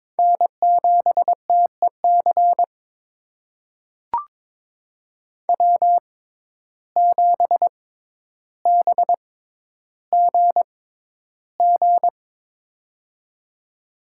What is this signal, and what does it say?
Morse code, heard as a keyed tone, about 22 wpm, spaced as for 11 wpm, 700 Hz. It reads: N7TEC W7BGG
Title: USA Call Signs - 6x Character Spacing - 22wpm